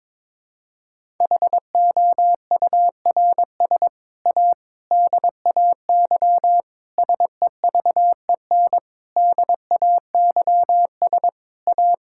Morse code, heard as a keyed tone, about 22 wpm, 700 Hz. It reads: HOURS A DAY SEVEN DAYS A
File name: practice / 22wpm / Sets of 7 Words - 22wpm